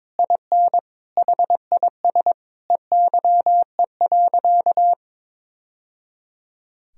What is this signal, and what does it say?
Morse code, heard as a keyed tone, about 22 wpm, 700 Hz. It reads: IN HIS EYE.
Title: Sentences from Top 1000 Words - Review Part 1 - 22wpm